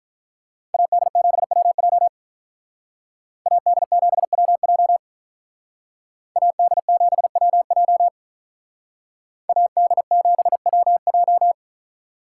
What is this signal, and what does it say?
Morse code, heard as a keyed tone, about 42 wpm, 700 Hz. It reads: AD7WJ AD7WJ AD7WJ AD7WJ
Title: Speed Racing - USA 2x2 Call Signs 35wpm